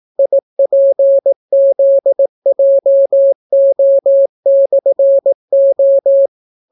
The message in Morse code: IPZJO/O

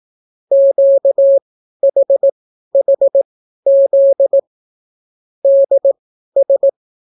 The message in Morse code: QHHZ DS

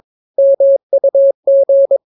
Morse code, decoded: MUG

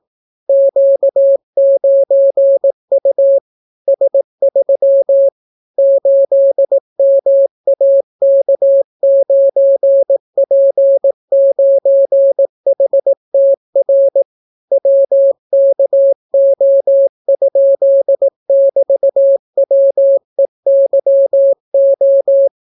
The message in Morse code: Q9U S3 8MAK9P9HTR WKO?<BT>WEYO

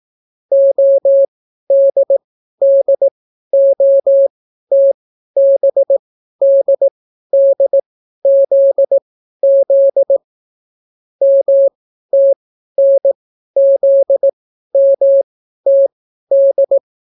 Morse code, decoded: ODDOTBDDZZ MTNZMTD

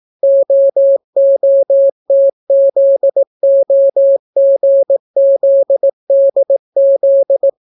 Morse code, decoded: OOTZOGZDZ

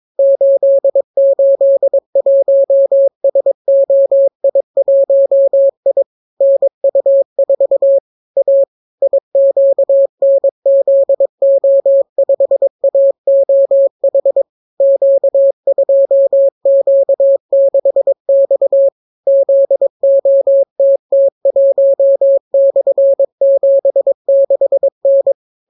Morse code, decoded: 881SOI1I NU4 A IQNZO5AOH Q2Q6X ZOTT1/76N

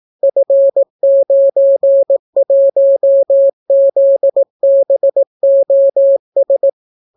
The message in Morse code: F91ZBOS

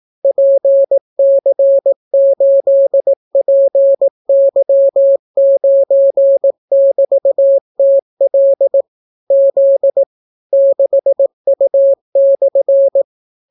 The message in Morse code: PC8PY9<BT>TL Z 6U/

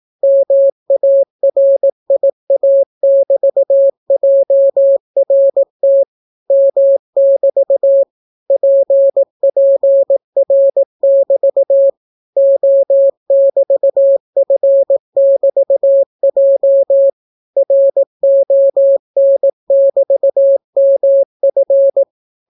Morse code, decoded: MARIA<BT>JRT M<BT> PPR<BT> O<BT>F<BT>J RON<BT>MF

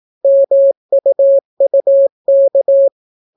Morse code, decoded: MUUK

